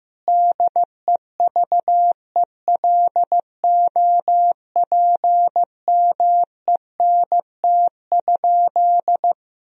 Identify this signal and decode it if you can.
Morse code, heard as a keyed tone, about 15 wpm, 700 Hz. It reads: DEVELOPMENT?
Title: Sentences from Top 1000 Words - Review Part 4 - 15wpm